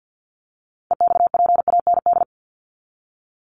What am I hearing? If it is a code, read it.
Morse code, heard as a keyed tone, about 50 wpm, 700 Hz. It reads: EXPAND
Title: Top 1251-1500 Words 50wpm